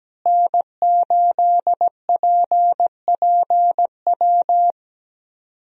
Morse code, heard as a keyed tone, about 17 wpm, 700 Hz. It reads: N8PPW
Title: USA Call Signs - 5 Characters - Repeated 3x - 17wpm